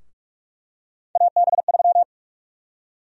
Morse code, heard as a keyed tone, about 44 wpm, 700 Hz. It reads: AB3
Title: USA Call Sign Prefixes - 45wpm